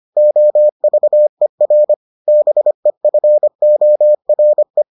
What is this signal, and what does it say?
Morse code, heard as a keyed tone, about 25 wpm, 600 Hz. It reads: OVER BEFORE